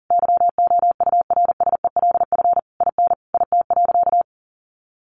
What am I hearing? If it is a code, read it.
Morse code, heard as a keyed tone, about 40 wpm, 700 Hz. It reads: YOURSELF IN IT.
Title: Sentences from Top 1300 Words 40wpm